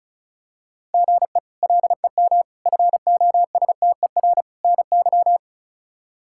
Morse code, heard as a keyed tone, about 36 wpm, 700 Hz. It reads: GE LEM FOSTER NY
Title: K1USN SST QSO Practice - 35wpm